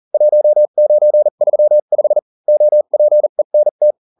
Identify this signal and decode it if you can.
Morse code, heard as a keyed tone, about 40 wpm, spaced as for 35 wpm, 600 Hz. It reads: 1935 OPENT